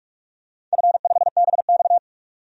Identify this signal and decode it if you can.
Morse code, heard as a keyed tone, about 46 wpm, 700 Hz. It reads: F5BX
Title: DXCC-100 Call Signs - 4 Characters - 45wpm